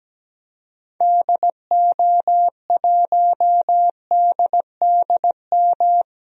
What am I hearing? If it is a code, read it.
Morse code, heard as a keyed tone, about 17 wpm, 700 Hz. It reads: DO1DDM
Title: DXCC-100 Call Signs - 6 Characters - 17wpm